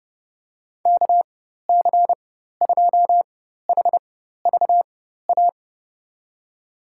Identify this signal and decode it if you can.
Morse code, heard as a keyed tone, about 30 wpm, spaced as for 14 wpm, 700 Hz. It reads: KC2HVA
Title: USA Call Signs - 6 Characters - 4x Character Spacing - 30wpm